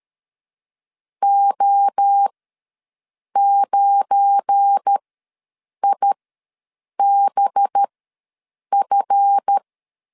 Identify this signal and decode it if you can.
Morse code, heard as a keyed tone, about 12 wpm, spaced as for 7 wpm, 800 Hz. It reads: O9IBF
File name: LEZIONI / 7 GF9Z / 7 GF9 ws 12wpm 3344